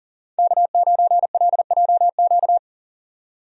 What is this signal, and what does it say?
Morse code, heard as a keyed tone, about 40 wpm, 700 Hz. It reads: K9LJQ
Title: USA Call Signs - 5 Characters - 40wpm